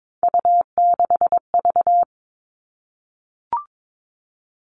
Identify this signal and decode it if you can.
Morse code, heard as a keyed tone, about 22 wpm, 700 Hz. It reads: U6V
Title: All Cut Numbers - 3-Digit 22wpm